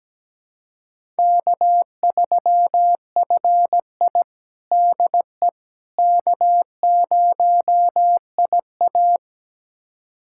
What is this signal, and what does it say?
Morse code, heard as a keyed tone, about 17 wpm, 700 Hz. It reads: K3FI DE K0IA
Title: USA Callsign DE Callsign - 4 Characters - 17wpm